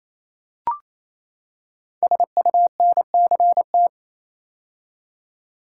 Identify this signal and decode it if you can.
Morse code, heard as a keyed tone, about 27 wpm, 700 Hz. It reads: SUNCT
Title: Code Groups - 5 Letter:Numbers - 28wpm